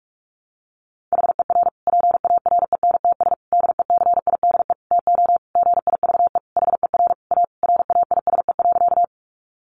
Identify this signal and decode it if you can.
Morse code, heard as a keyed tone, about 44 wpm, 700 Hz. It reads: HER PARENTS DECIDE TO GIVE HER A RAISE.
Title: Easy Phrases 45wpm